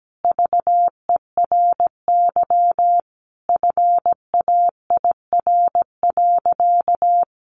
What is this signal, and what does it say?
Morse code, heard as a keyed tone, about 17 wpm, 700 Hz. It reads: VERY FAIR.
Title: Sentences from Top 1300 Words 17wpm